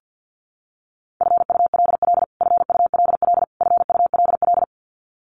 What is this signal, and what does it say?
Morse code, heard as a keyed tone, about 50 wpm, 700 Hz. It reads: FULL FULL FULL
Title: Top 500 Words Repeated 50wpm